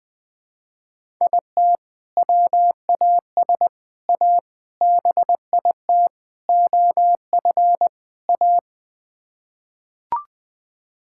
Morse code, heard as a keyed tone, about 20 wpm, 700 Hz. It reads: IT WAS A BIT OF A
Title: Sets of 6 Words - 20wpm